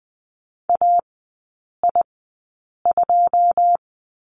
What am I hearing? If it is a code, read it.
Morse code, heard as a keyed tone, about 19 wpm, spaced as for 8 wpm, 700 Hz. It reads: AI2
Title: 100 Lesson 17e - Letter u - Three Characters 20wpm